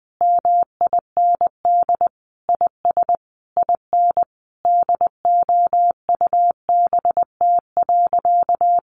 Morse code, heard as a keyed tone, about 20 wpm, 700 Hz. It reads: MIND IS IN DOUBT.
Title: Sentences from Top 1500 Words 20wpm